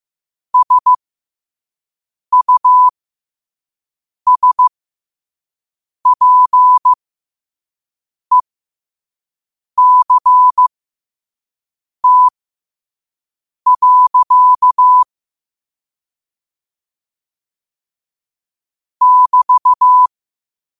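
Morse code, heard as a keyed tone, about 15 wpm, spaced as for 5 wpm, 1000 Hz. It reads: SUSPECT. <BT>
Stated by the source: Morse code sent at 5 wpm.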